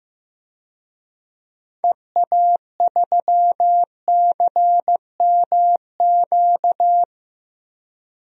Morse code, heard as a keyed tone, about 15 wpm, 690 Hz. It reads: EA3CMQ
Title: DXCC-100 Call Signs - 6 Characters - 15wpm